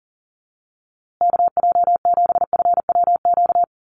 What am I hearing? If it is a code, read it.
Morse code, heard as a keyed tone, about 39 wpm, 700 Hz. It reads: KJ7FWQ
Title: USA Call Signs Repeated 2x 40wpm